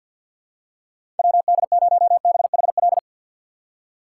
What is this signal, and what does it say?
Morse code, heard as a keyed tone, about 47 wpm, 700 Hz. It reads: WD0BHL
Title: USA Call Signs - 6 Characters - 3x Character Spacing - 50wpm